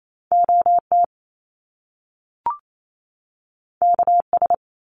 Morse code, heard as a keyed tone, about 28 wpm, 700 Hz. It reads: OT KS
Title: Top Two Letter Combinations in Words 28wpm